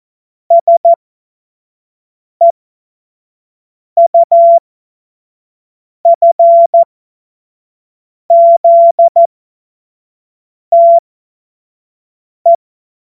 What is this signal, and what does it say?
Morse code, heard as a keyed tone, about 14 wpm, spaced as for 5 wpm, 680 Hz. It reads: SEUFZTE
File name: Witz-12-August-2025-25-F70